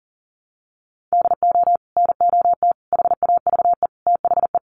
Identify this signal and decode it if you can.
Morse code, heard as a keyed tone, about 40 wpm, 700 Hz. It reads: DO NOT HAVE THE
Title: Sets of 4 Words 40wpm